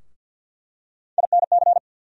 Morse code, heard as a keyed tone, about 49 wpm, 700 Hz. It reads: INC
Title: USA Call Sign Postfixes - 3 Letters 50wpm